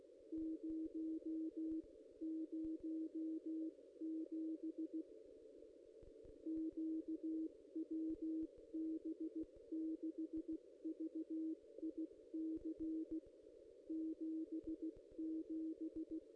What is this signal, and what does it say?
Morse code, heard as a keyed tone, about 16 wpm, spaced as for 14 wpm, 320 Hz. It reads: 007 Q WB6VIC 77
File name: contests / ss-cw-k5tr-op-n5rz -1600